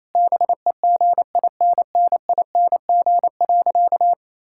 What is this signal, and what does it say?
Morse code, heard as a keyed tone, about 28 wpm, 700 Hz. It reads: BEGINNING.
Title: Sentences from Top 1000 Words - Review Part 5 - 28wpm